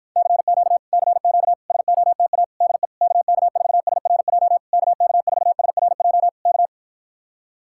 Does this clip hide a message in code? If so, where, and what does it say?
Yes: Morse code from 0.2 to 6.7 s, heard as a keyed tone, about 54 wpm, 700 Hz. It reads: CQ CQ SOTA DE KK4SRJ KK4SRJ K